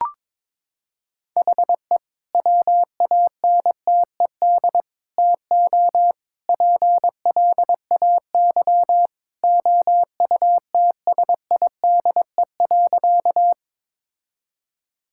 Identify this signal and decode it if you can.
Morse code, heard as a keyed tone, about 22 wpm, 700 Hz. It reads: HE WANTED TO PLAY OUTSIDE.
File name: practice / 22wpm / Sentences with 3-Word Phrases - 22wpm